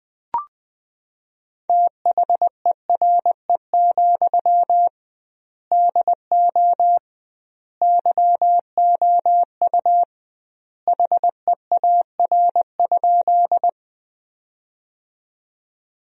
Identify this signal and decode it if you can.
Morse code, heard as a keyed tone, about 20 wpm, 700 Hz. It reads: THERE, DO YOU HEAR?
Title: 2x Word-Spacing - Sentences from Top 300 Words - 20wpm